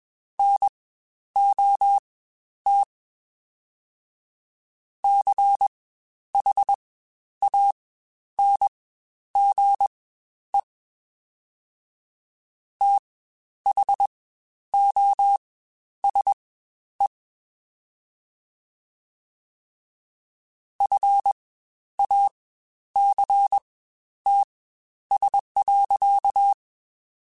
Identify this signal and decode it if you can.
Morse code, heard as a keyed tone, about 21 wpm, spaced as for 10 wpm, 800 Hz. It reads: NOT CHANGE THOSE FACTS.
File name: Quotes Spaced / Quotes Spaced-mp3s / IgnoringFacts sp @20